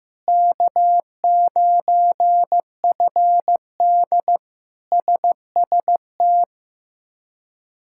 Morse code, heard as a keyed tone, about 15 wpm, 700 Hz. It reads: K9FD SST
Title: K1USN SST QSO Practice - 15wpm